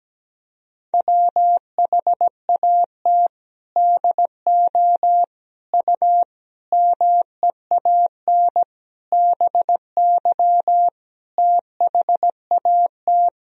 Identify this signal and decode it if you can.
Morse code, heard as a keyed tone, about 17 wpm, 700 Hz. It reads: WHAT DO U MEAN BY THAT
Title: Sentences from Top 400 Words and Abbreviations 17wpm